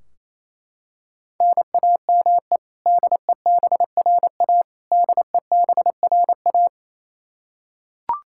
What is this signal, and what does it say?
Morse code, heard as a keyed tone, about 27 wpm, 700 Hz. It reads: NAME DEBRA DEBRA
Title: Practice QSO - Exchange 1 - Beginning - 28wpm